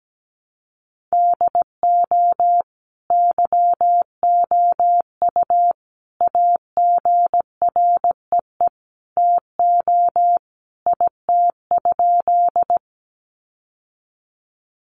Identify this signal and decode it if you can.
Morse code, heard as a keyed tone, about 17 wpm, 700 Hz. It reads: DO YOU AGREE TO IT?
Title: Sentences from Top 1100 Words 17wpm